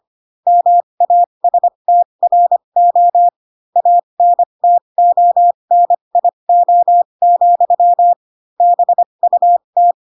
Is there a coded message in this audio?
Yes: Morse code from 0.5 to 9.9 s, heard as a keyed tone, about 25 wpm, spaced as for 22 wpm, 700 Hz. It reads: MASTRO ANTONIO, BUT